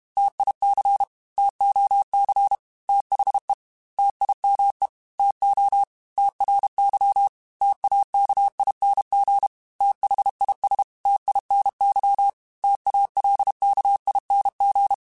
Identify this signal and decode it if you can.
Morse code, heard as a keyed tone, about 31 wpm, 800 Hz. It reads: TIC TOC THE TIME TO TRY TAKING THIS TINY TALKING